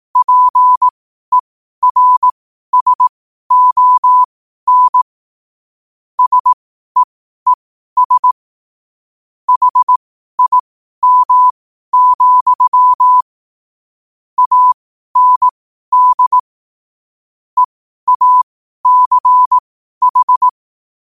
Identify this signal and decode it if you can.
Morse code, heard as a keyed tone, about 18 wpm, spaced as for 12 wpm, 1000 Hz. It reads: PERSON SEES HIM, AND EACH
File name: qotdfiles / 12wpm